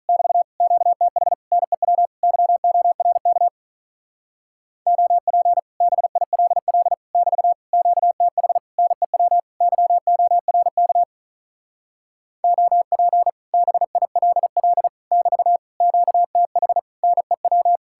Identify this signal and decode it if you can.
Morse code, heard as a keyed tone, about 44 wpm, 700 Hz. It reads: <BT> QTH NEW YORK OP BILL <BT> QTH NEW YORK OP BILL <BT> QTH NEW